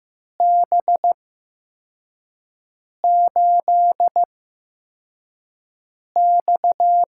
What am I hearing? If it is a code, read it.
Morse code, heard as a keyed tone, about 15 wpm, spaced as for 4 wpm, 700 Hz. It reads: B8X
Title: USA Call Signs - 8x Character Spacing - 15wpm